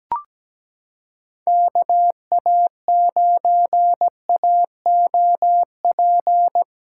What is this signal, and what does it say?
Morse code, heard as a keyed tone, about 17 wpm, 700 Hz. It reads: KA9AOP
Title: USA Callsign DE Callsign - 6 Characters - 17wpm